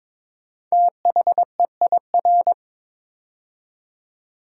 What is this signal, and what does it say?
Morse code, heard as a keyed tone, about 22 wpm, 700 Hz. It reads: THEIR
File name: 5 Letter Words - First 3 Letters Given - 22wpm